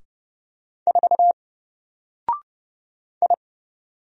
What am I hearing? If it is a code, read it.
Morse code, heard as a keyed tone, about 30 wpm, 700 Hz. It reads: 4 I
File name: Single-Letter-Number-30wpm